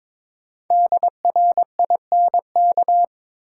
Morse code, heard as a keyed tone, about 22 wpm, 690 Hz. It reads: DRINK
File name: 5 Letter Words - First 3 Letters Given - 22wpm